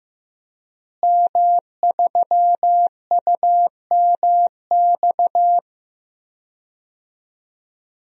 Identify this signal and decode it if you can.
Morse code, heard as a keyed tone, about 15 wpm, 700 Hz. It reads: M3UMX
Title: DXCC-100 Call Signs - 5 Characters - Repeated 3x - 15wpm